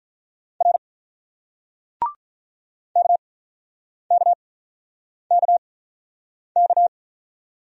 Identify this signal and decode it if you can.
Morse code, heard as a keyed tone, about 48 wpm, spaced as for 9 wpm, 700 Hz. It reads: R KKKK